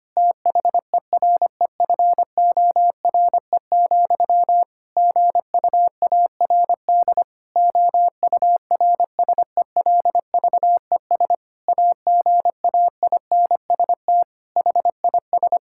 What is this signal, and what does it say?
Morse code, heard as a keyed tone, about 25 wpm, 700 Hz. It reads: THEREFORE, GUARD OURSELVES AGAINST HIS
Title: Sentences from Top 1600 Words 25wpm